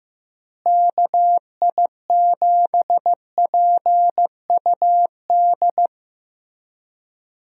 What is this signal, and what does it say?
Morse code, heard as a keyed tone, about 15 wpm, 700 Hz. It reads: KI7PUD
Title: USA Call Signs - 6 Characters - Repeated 2x - 15wpm